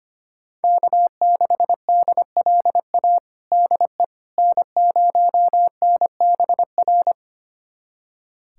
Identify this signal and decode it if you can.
Morse code, heard as a keyed tone, about 25 wpm, 700 Hz. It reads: K6DLA DE N0NBR
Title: USA Callsign DE Callsign - 5 Characters - 25wpm